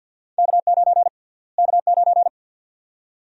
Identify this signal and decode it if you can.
Morse code, heard as a keyed tone, about 47 wpm, 700 Hz. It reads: K9 K9
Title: USA Call Sign Prefixes - Repeated 2x - 50wpm